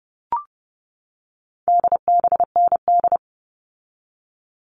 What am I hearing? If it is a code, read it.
Morse code, heard as a keyed tone, about 30 wpm, 700 Hz. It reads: DBND
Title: All Cut Numbers - 4-Digit 30wpm